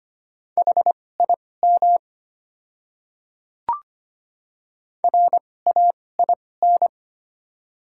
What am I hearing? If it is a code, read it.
Morse code, heard as a keyed tone, about 25 wpm, spaced as for 18 wpm, 700 Hz. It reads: HIM RAIN
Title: 2x Character Spacing - Top 300 Words - 25wpm